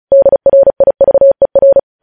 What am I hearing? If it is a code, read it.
Morse code, heard as a keyed tone, about 36 wpm, 550 Hz. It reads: DRIVER
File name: podcast / News headlines-1x-35x35